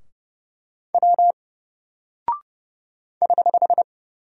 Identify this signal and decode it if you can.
Morse code, heard as a keyed tone, about 31 wpm, 700 Hz. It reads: W <HH>
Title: Single Character (Letter:Number:Punctuation:Prosign) Alt 30wpm